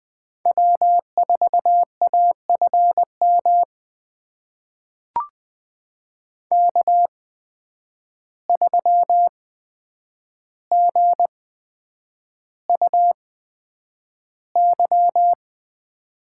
Morse code, heard as a keyed tone, about 20 wpm, spaced as for 9 wpm, 700 Hz. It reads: W4AFM K3GUY
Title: USA Call Signs - 5 Characters - 8x Character Spacing - 20wpm